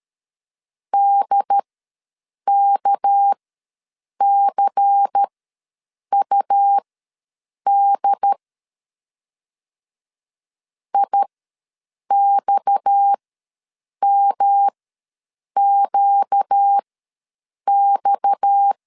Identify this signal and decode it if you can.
Morse code, heard as a keyed tone, about 13 wpm, spaced as for 7 wpm, 800 Hz. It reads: DKCUD IXMQX
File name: LEZIONI / 6 PKX0 / 6 PKX ws 12wpm 3106